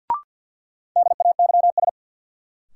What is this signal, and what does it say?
Morse code, heard as a keyed tone, about 51 wpm, 700 Hz. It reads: DAYS
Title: Sets of 5 Words Encore - New Word Drill - 50wpm